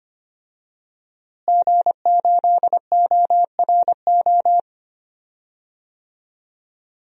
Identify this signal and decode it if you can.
Morse code, heard as a keyed tone, about 25 wpm, 700 Hz. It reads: G8ORO
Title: DXCC-100 Call Signs - 5 Characters - Repeated 3x - 25wpm